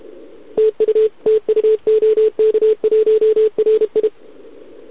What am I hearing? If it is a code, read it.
Morse code, heard as a keyed tone, about 32 wpm, 440 Hz, with radio noise. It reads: TU TU OK1RI
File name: ARCHIV / OKDX2020 / OK1RIb